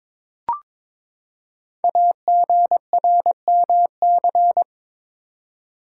Morse code, heard as a keyed tone, about 22 wpm, 700 Hz. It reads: AGRMC
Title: Code Groups - 5 Letters - 22wpm